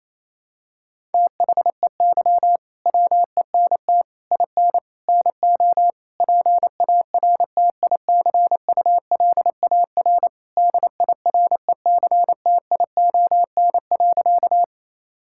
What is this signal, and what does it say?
Morse code, heard as a keyed tone, about 28 wpm, 700 Hz. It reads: THEY WENT IN NO PARTICULAR DIRECTION.
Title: Sentences from Top 1000 Words - Review Part 5 - 28wpm